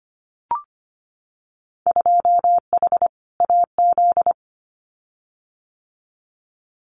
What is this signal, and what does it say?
Morse code, heard as a keyed tone, about 25 wpm, 700 Hz. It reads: 2H AZ
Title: Winter Field Day Exchanges 25wpm